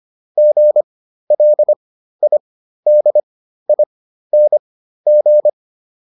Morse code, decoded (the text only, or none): GLIDING